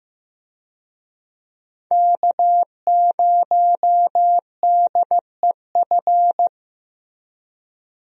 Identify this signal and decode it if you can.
Morse code, heard as a keyed tone, about 15 wpm, 700 Hz. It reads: K0DEF
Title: USA Call Signs - 5 Characters - Repeated 2x - 15wpm